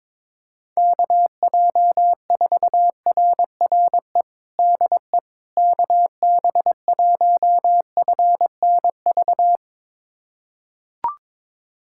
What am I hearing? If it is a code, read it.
Morse code, heard as a keyed tone, about 22 wpm, 700 Hz. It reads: KJ4RRE DE KB1FNV
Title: USA Callsign DE Callsign - 6 Characters - 22wpm